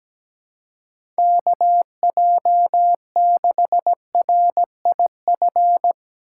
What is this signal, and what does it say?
Morse code, heard as a keyed tone, about 17 wpm, 700 Hz. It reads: KJ6RIF
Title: USA Call Signs - 6 Characters - 3x Character Spacing - 17wpm